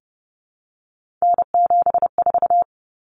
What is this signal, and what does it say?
Morse code, heard as a keyed tone, about 30 wpm, 700 Hz. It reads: N74
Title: Common Cut Numbers - 3-Digit 30wpm